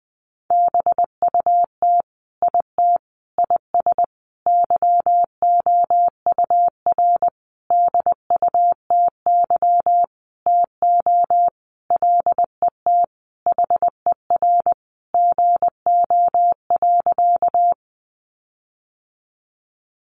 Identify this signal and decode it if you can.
Morse code, heard as a keyed tone, about 20 wpm, 700 Hz. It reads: BUT IT IS YOUR DUTY TO LET HER GO.
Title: Sentences from Top 1500 Words 20wpm